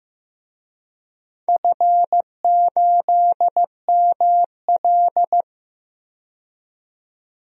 Morse code, heard as a keyed tone, about 15 wpm, 700 Hz. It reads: F8ML
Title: DXCC-100 Call Signs - 4 Characters - 15wpm